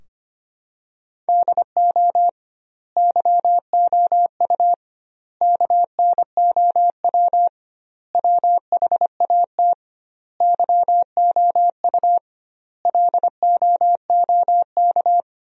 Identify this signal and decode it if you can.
Morse code, heard as a keyed tone, about 25 wpm, 700 Hz. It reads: DO YOU KNOW WHAT YOU LOOK
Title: 2x Word-Spacing - Sentences from Top 100 Words - 25wpm